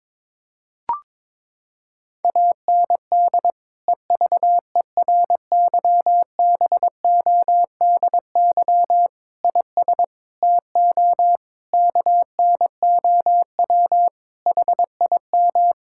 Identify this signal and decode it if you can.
Morse code, heard as a keyed tone, about 22 wpm, 700 Hz. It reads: AND EVERYBODY IS TO KNOW HIM
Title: Sentences from Top 1600 Words 22wpm